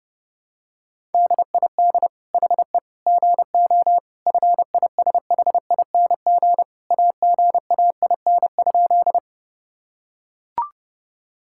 Morse code, heard as a keyed tone, about 32 wpm, 700 Hz. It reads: DID HE GO FISHING AGAIN?
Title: Sentences from Top 1000 Words - Review Part 5 - 30wpm